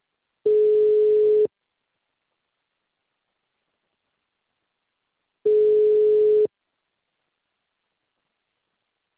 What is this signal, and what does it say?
Ringback tone (Europe, 425 Hz).
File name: out- -2006- - - .30752